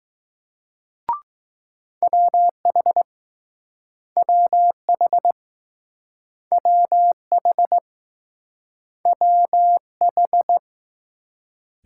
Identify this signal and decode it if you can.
Morse code, heard as a keyed tone, about 20 wpm, 700 Hz. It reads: WH WH WH WH